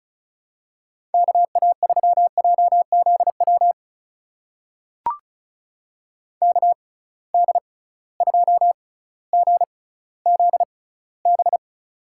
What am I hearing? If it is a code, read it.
Morse code, heard as a keyed tone, about 36 wpm, spaced as for 18 wpm, 700 Hz. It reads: KA3JZW KD2GZB
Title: USA Call Signs - 6 Characters - 6x Character Spacing - 35wpm